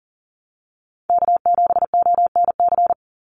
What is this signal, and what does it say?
Morse code, heard as a keyed tone, about 40 wpm, 700 Hz. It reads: K7ONC
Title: USA Call Signs 40wpm